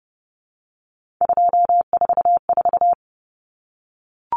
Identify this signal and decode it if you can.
Morse code, heard as a keyed tone, about 30 wpm, 700 Hz. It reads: 244
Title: Numbers - 3-Digit 30wpm